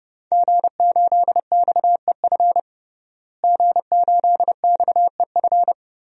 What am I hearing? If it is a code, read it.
Morse code, heard as a keyed tone, about 30 wpm, 700 Hz. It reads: G8XEF G8XEF
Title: DXCC-100 Call Signs - 5 Characters - Repeated 2x - 30wpm